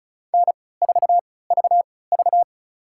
Morse code, heard as a keyed tone, about 34 wpm, spaced as for 20 wpm, 700 Hz. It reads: N4VV
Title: USA Call Signs - 4 Characters - 3x Character Spacing - 35wpm